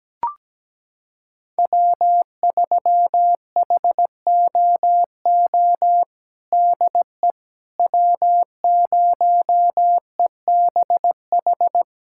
Morse code, heard as a keyed tone, about 17 wpm, 700 Hz. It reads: W3HOO DE W0EBH
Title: USA Callsign DE Callsign - 5 Characters - 17wpm